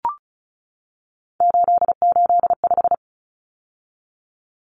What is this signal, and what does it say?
Morse code, heard as a keyed tone, about 34 wpm, 700 Hz. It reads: 885
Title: Numbers - 3-Digit 35wpm